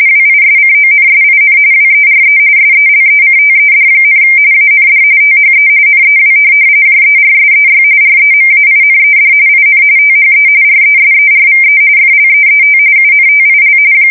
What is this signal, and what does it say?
RTTY, about 45.45 baud. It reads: WISCONSIN WERE SELECTED, AND THE NEW JERSEY REMAINED IN THE RESERVE FLEET UNTIL
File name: fsk170 / USS-New-Jersey